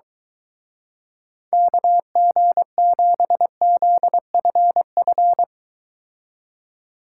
Morse code, heard as a keyed tone, about 23 wpm, 700 Hz. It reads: KG7ZFF